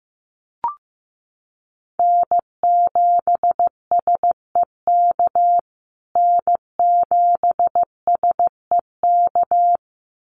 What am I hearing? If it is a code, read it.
Morse code, heard as a keyed tone, about 15 wpm, 700 Hz. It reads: N7SEK N7SEK
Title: USA Call Signs Repeated 2x 15wpm